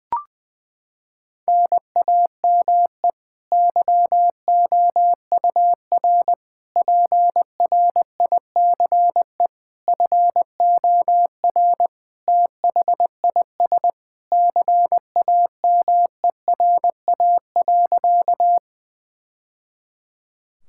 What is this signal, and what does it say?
Morse code, heard as a keyed tone, about 20 wpm, 700 Hz. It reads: NAME YOUR PRICE FOR THIS CAMERA.
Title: Sentences from Top 1000 Words - Review Part 5 - 20wpm